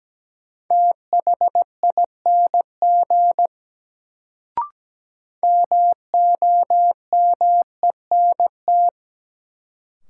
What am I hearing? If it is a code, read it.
Morse code, heard as a keyed tone, about 17 wpm, 700 Hz. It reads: THING MOMENT
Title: Sets of 3 Words - New Word Drill - 17wpm